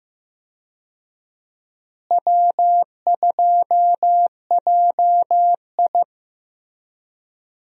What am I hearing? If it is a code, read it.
Morse code, heard as a keyed tone, about 15 wpm, 700 Hz. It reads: W2JI
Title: USA Call Signs - 4 Characters - Repeated 2x - 15wpm